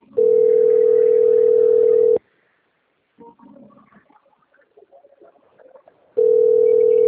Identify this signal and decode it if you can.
Ringback tone (North America), 440 + 480 Hz.
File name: out- - - - - .23618